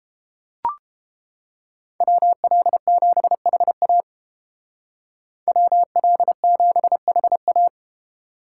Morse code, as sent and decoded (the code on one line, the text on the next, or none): .-- .-.. --... .... .- / .-- .-.. --... .... .-
WL7HA WL7HA